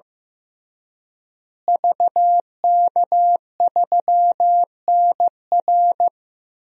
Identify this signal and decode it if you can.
Morse code, heard as a keyed tone, about 15 wpm, 700 Hz. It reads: VK3NR
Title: DXCC-100 Call Signs - 5 Characters - Repeated 3x - 15wpm